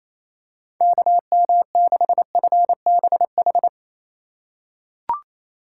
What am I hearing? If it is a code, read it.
Morse code, heard as a keyed tone, about 28 wpm, 700 Hz. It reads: KM6FBH
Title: USA Call Signs - 6 Characters - 2x Character Spacing - 28wpm